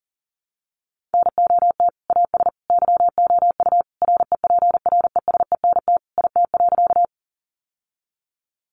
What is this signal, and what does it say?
Morse code, heard as a keyed tone, about 40 wpm, 700 Hz. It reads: NOT AS YOU REPRESENT IT.
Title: Sentences from Top 1200 Words 40wpm